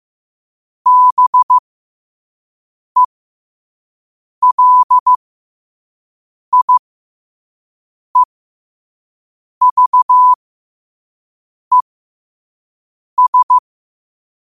Morse code, heard as a keyed tone, about 15 wpm, spaced as for 5 wpm, 1000 Hz. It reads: BELIEVES
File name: qotdfiles / 5wpm